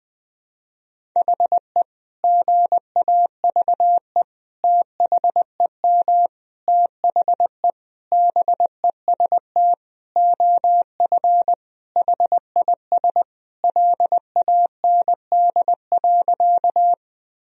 Morse code, heard as a keyed tone, about 20 wpm, 700 Hz. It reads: HE GAVE THEM THE BEST OF HIS LAND.
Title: Sentences from Top 1000 Words - Review Part 1 - 20wpm